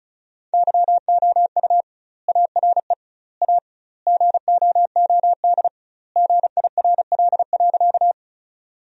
Morse code, heard as a keyed tone, about 36 wpm, 700 Hz. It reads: YOU ARE A GOOD GIRL.
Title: 2x Word-Spacing - Sentences from Top 400 Words - 35wpm